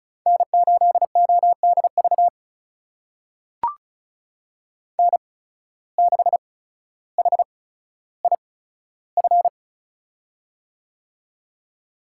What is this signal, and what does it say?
Morse code, heard as a keyed tone, about 36 wpm, spaced as for 15 wpm, 700 Hz. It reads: N8ODV N6HIF
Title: USA Call Signs - 5 Characters - 8x Character Spacing - 35wpm